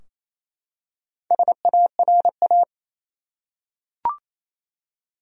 Morse code, heard as a keyed tone, about 27 wpm, 700 Hz. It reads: SARA
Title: USA Common Names - Max 5 Letters - 28wpm